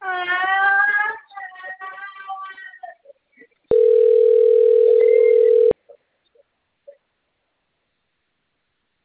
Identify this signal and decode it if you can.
Ringback tone (North America), 440 + 480 Hz.
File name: out- - - - - .85923